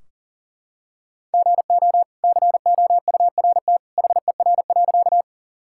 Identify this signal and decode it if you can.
Morse code, heard as a keyed tone, about 40 wpm, 700 Hz. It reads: GO COURT HER.
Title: Sentences from Top 1600 Words 40wpm